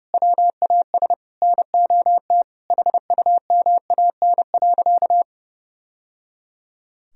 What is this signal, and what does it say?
Morse code, heard as a keyed tone, about 30 wpm, 700 Hz. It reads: WAS NOT HUMAN.
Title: Sentences from Top 1000 Words - Review Part 3 - 30wpm